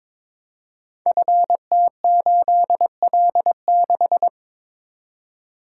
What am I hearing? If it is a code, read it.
Morse code, heard as a keyed tone, about 22 wpm, 700 Hz. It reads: FT8L6
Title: Code Groups - 5 Letter:Numbers - 22wpm